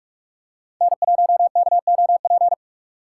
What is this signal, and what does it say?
Morse code, heard as a keyed tone, about 44 wpm, 700 Hz. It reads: N1KOP
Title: USA Call Signs - 5 Characters - 6x Character Spacing - 45wpm